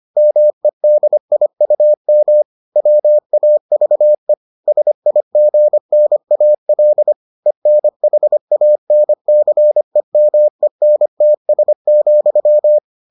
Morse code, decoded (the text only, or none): MEDIUM WAVE SIGNAL ENHANCEMENTS,